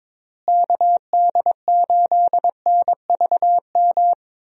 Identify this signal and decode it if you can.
Morse code, heard as a keyed tone, about 22 wpm, 700 Hz. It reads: KD8NVM
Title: USA Call Signs - 6 Characters - Repeated 2x - 22wpm